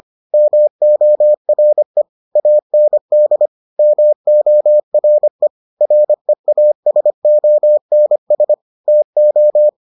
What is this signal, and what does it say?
Morse code, heard as a keyed tone, about 25 wpm, 600 Hz. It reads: MORE AND MORE REASONS TO